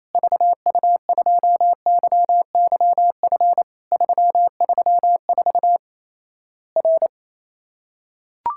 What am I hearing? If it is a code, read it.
Morse code, heard as a keyed tone, about 27 wpm, 700 Hz. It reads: VU2YYF 334 R
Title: Worked All Europe DX Contest - QTC Exchange - 28wpm